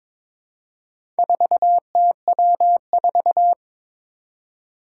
Morse code, heard as a keyed tone, about 22 wpm, 700 Hz. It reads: 4TW4
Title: Four Challenging Letter-Number Combinations 22wpm